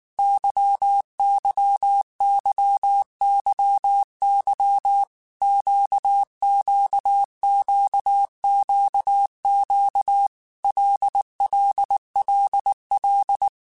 Morse code, heard as a keyed tone, about 19 wpm, 800 Hz. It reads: YYYYY QQQQQ LLLL
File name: Groups@18 / Groups@18 mp3 / LFCYQ@18